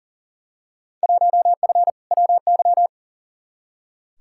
Morse code, heard as a keyed tone, about 37 wpm, 700 Hz. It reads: 1F WY
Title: Field Day Exchanges Encore - 40wpm